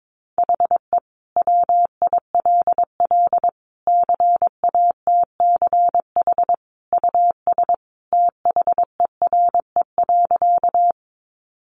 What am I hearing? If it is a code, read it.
Morse code, heard as a keyed tone, about 22 wpm, 700 Hz. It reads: HE WILL CATCH US THERE.
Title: Sentences from Top 1500 Words 22wpm